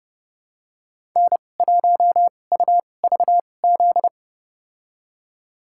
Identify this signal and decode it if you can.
Morse code, heard as a keyed tone, about 30 wpm, spaced as for 22 wpm, 700 Hz. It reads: N1UVZ
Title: USA Call Signs - 2x Character Spacing - 30wpm